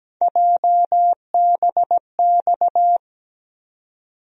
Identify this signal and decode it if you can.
Morse code, heard as a keyed tone, about 17 wpm, 700 Hz. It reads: JBX
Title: USA Call Sign Postfixes - 3 Letters 17wpm